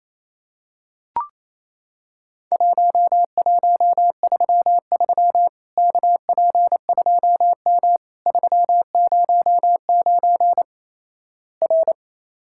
Morse code, heard as a keyed tone, about 28 wpm, 700 Hz. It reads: 1133 KP2M 309 R
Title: Worked All Europe DX Contest - QTC Exchange - 28wpm